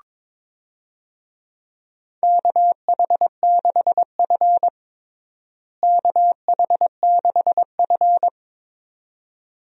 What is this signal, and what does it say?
Morse code, heard as a keyed tone, about 22 wpm, 700 Hz. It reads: KH6F KH6F
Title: USA Call Signs - 4 Characters - Repeated 2x - 22wpm